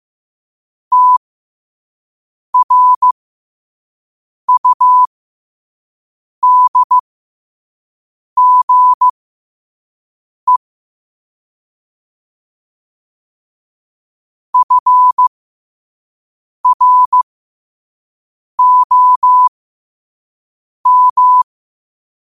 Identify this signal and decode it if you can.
Morse code, heard as a keyed tone, about 15 wpm, spaced as for 5 wpm, 1000 Hz. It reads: TRUDGE FROM
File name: qotdfiles / 5wpm